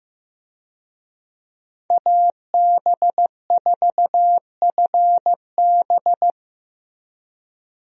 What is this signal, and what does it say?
Morse code, heard as a keyed tone, about 14 wpm, 700 Hz. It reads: AB4FB
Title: USA Call Signs - 5 Characters - Repeated 3x - 15wpm